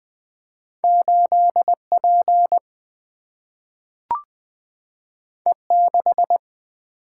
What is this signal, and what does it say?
Morse code, heard as a keyed tone, about 20 wpm, 700 Hz. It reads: 8 P E 6
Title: Single-Letter Single-Number 20wpm